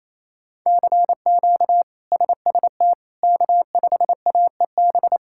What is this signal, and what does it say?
Morse code, heard as a keyed tone, about 27 wpm, 700 Hz. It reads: CQ SST K5AEB
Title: K1USN SST QSO Practice - 28wpm